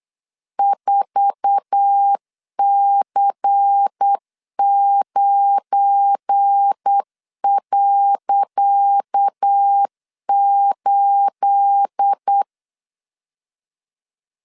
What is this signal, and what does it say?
Morse code, heard as a keyed tone, about 8 wpm, 800 Hz. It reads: 4C9.8
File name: LEZIONI / 10 SIGNS / 10C DOT 8wpm 3649